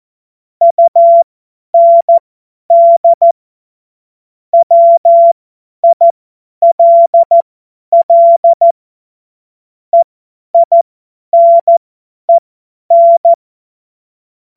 Morse code, heard as a keyed tone, about 14 wpm, spaced as for 10 wpm, 680 Hz. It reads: UND WILL EINEN
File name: Morsekurs / Witze / Witz 02. Mai 2023 50 F70